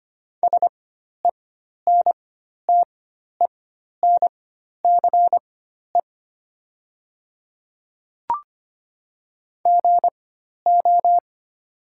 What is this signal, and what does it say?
Morse code, heard as a keyed tone, about 25 wpm, spaced as for 12 wpm, 690 Hz. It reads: SENTENCE GO
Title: 4x Character Spacing - Top 200 Words - 25wpm